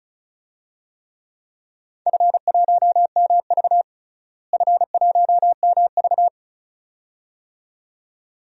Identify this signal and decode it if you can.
Morse code, heard as a keyed tone, about 34 wpm, 690 Hz. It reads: F1MV F1MV
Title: DXCC-100 Call Signs - 4 Characters - Repeated 2x - 35wpm